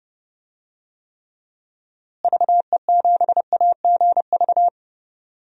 Morse code, heard as a keyed tone, about 30 wpm, 700 Hz. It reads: VE7AGV
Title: DXCC-100 Call Signs - 6 Characters - Repeated 3x - 30wpm